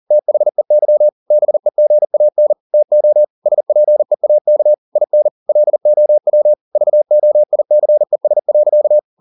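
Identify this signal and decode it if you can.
Morse code, heard as a keyed tone, about 40 wpm, 600 Hz. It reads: THEY BEGAN TO SPEAK IN LOW VOICES.
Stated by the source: Morse code at 40 wpm, practice text, CW med Gnister.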